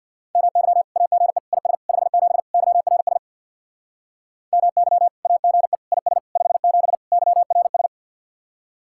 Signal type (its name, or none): Morse code